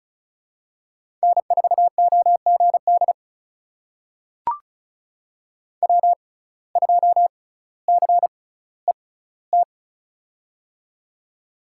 Morse code, heard as a keyed tone, about 36 wpm, spaced as for 18 wpm, 700 Hz. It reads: N4OGD W2CET
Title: USA Call Signs - 5 Characters - 6x Character Spacing - 35wpm